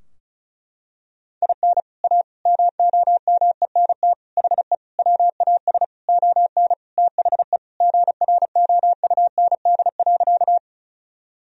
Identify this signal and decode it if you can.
Morse code, heard as a keyed tone, about 34 wpm, 700 Hz. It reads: IN A MOMENT HE WAS ON THE GROUND.
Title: Sentences from Top 1000 Words - Review Part 2 - 35wpm